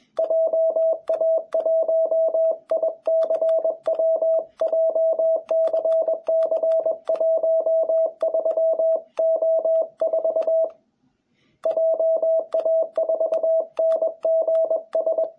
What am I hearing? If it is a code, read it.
Morse code, heard as a keyed tone, about 21 wpm, 660 Hz. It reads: JA1ICW JCC13O4 JA4NGS
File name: jh8cbh / 2021cw